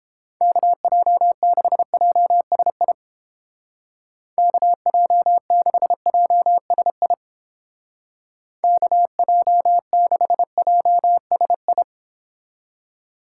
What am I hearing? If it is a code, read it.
Morse code, heard as a keyed tone, about 31 wpm, 700 Hz. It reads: KJ6JSI KJ6JSI KJ6JSI